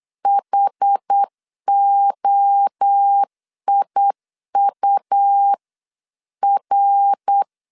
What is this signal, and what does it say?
Morse code, heard as a keyed tone, about 8 wpm, 800 Hz. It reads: HOIUR
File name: CW / DET / 1 3 1500C S C 8wpm 4214